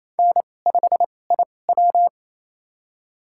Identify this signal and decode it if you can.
Morse code, heard as a keyed tone, about 28 wpm, spaced as for 20 wpm, 700 Hz. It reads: N5IW
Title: USA Call Signs - 4 Characters - 2x Character Spacing - 28wpm